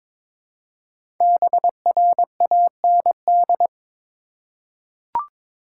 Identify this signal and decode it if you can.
Morse code, heard as a keyed tone, about 22 wpm, 700 Hz. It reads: BRAND
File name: 5 Letter Words - First 3 Letters Given - 22wpm